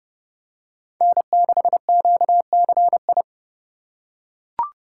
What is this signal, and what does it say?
Morse code, heard as a keyed tone, about 30 wpm, 700 Hz. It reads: N6QCI
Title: USA Call Signs - 5 Characters - 6x Character Spacing - 30wpm